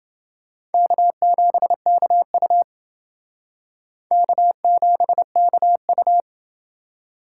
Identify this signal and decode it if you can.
Morse code, heard as a keyed tone, about 28 wpm, 700 Hz. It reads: K7KU K7KU